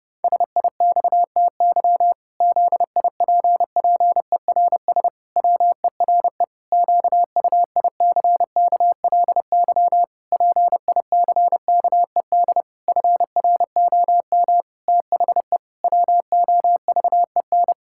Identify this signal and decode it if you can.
Morse code, heard as a keyed tone, about 30 wpm, 700 Hz. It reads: SIXTY ZIPPERS WERE QUICKLY PICKED FROM THE WOVEN
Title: Pangram Challenge v2 - 30wpm